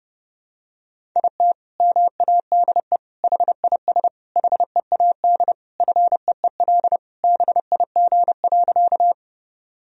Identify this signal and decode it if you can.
Morse code, heard as a keyed tone, about 30 wpm, 700 Hz. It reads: IT MADE HIS HEAD FEEL BIG.
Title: Sentences from Top 1000 Words - Review Part 1 - 30wpm